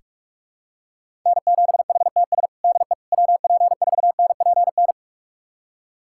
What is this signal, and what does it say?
Morse code, heard as a keyed tone, about 46 wpm, 700 Hz. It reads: N7HTS DE WP4NPN
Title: USA Callsign DE Callsign 45wpm